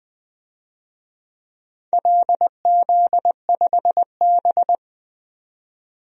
Morse code, heard as a keyed tone, about 20 wpm, 700 Hz. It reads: LZ5B
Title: DXCC-100 Call Signs - 4 Characters - 20wpm